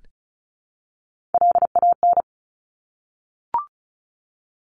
Morse code, heard as a keyed tone, about 36 wpm, 700 Hz. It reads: LAN
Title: Top Three Letter Combinations in Words 35wpm